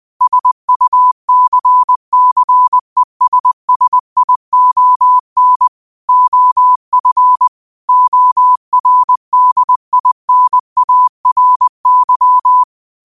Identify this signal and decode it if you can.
Morse code, heard as a keyed tone, about 20 wpm, 1000 Hz. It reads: SUCCESSION OF ORDINARY
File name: qotdfiles / 20wpm